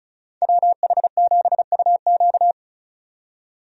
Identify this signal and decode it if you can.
Morse code, heard as a keyed tone, about 34 wpm, 700 Hz. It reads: WH7UQ
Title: USA Call Signs - 5 Characters - 35wpm